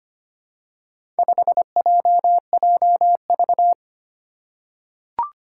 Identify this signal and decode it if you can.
Morse code, heard as a keyed tone, about 25 wpm, 700 Hz. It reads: 5JJV
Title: Four Challenging Letter-Number Combinations 25wpm